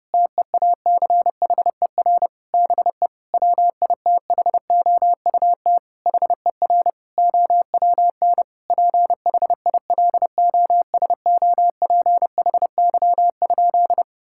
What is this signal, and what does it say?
Morse code, heard as a keyed tone, about 30 wpm, 700 Hz. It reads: TEACHER BE WITHOUT HER OWN PHILOSOPHY?
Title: Sentences from Top 1000 Words - Review Part 6 - 30wpm